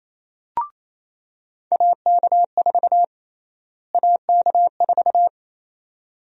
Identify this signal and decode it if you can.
Morse code, heard as a keyed tone, about 28 wpm, 700 Hz. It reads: AK4 AK4
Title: USA Call Sign Prefixes - Repeated 2x - 28wpm